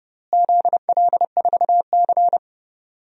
Morse code, heard as a keyed tone, about 32 wpm, 700 Hz. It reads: ZL4C
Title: DXCC-100 Call Signs - 4 Characters - Repeated 2x - 30wpm